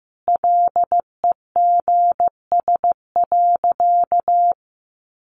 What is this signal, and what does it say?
Morse code, heard as a keyed tone, about 14 wpm, 700 Hz. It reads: LEGS.
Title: A Sound Of Thunder 15wpm